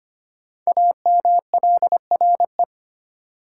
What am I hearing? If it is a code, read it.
Morse code, heard as a keyed tone, about 25 wpm, 700 Hz. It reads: AMLRE
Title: Code Groups - 5 Letters - 25wpm